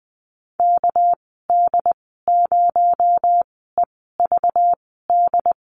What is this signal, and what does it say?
Morse code, heard as a keyed tone, about 19 wpm, spaced as for 14 wpm, 700 Hz. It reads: KD0EVD
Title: 238 Lesson 38g - Letter q - Call Signs 20wpm